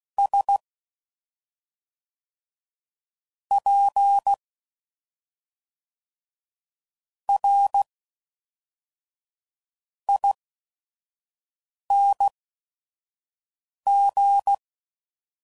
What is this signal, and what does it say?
Morse code, heard as a keyed tone, about 16 wpm, spaced as for 5 wpm, 800 Hz. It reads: S P R ING
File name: Suffixes / Special ING's / SlowToFaster ING's